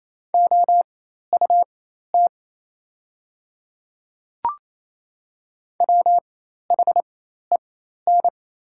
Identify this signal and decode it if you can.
Morse code, heard as a keyed tone, about 27 wpm, spaced as for 13 wpm, 700 Hz. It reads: OUT WHEN
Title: 4x Character Spacing - Top 100 Words - 28wpm